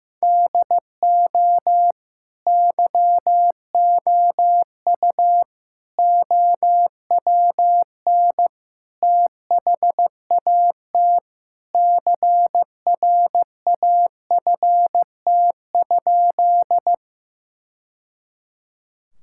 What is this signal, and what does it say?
Morse code, heard as a keyed tone, about 15 wpm, 700 Hz. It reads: DO YOU OWN THAT CRAFT?
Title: Sentences from Top 1000 Words - Review Part 6 - 15wpm